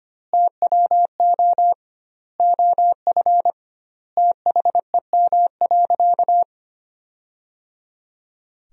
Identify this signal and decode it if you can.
Morse code, heard as a keyed tone, about 25 wpm, 700 Hz. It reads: TWO OF THEM.
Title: 2x Word-Spacing - Sentences from Top 100 Words - 25wpm